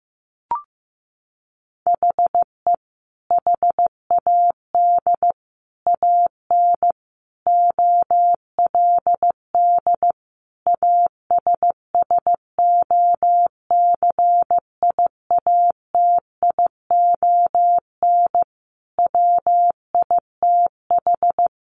Morse code, heard as a keyed tone, about 15 wpm, 700 Hz. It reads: HE HAD AN OLD ASSOCIATION WITH
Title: Sentences from Top 1100 Words 15wpm